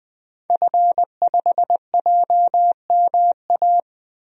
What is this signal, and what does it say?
Morse code, heard as a keyed tone, about 20 wpm, 700 Hz. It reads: F5JMA
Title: DXCC-100 Call Signs - 5 Characters - 20wpm